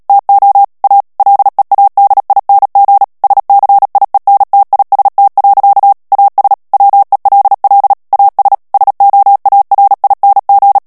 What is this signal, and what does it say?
Morse code, heard as a keyed tone, about 37 wpm, 800 Hz. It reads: TO A LEADING SCIENTIST. AS WELL AS SOARING
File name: morse mp3 / ANGLAIS texte clair 2 vitesse 35 mots minute